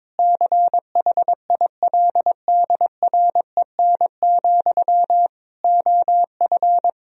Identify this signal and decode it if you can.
Morse code, heard as a keyed tone, about 22 wpm, 700 Hz. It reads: CHILDREN, OF
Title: Sentences from Top 1000 Words - Review Part 1 - 22wpm